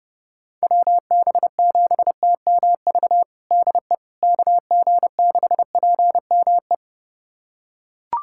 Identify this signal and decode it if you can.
Morse code, heard as a keyed tone, about 30 wpm, 700 Hz. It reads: WB7TMV DE KG6PME
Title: USA Callsign DE Callsign 30wpm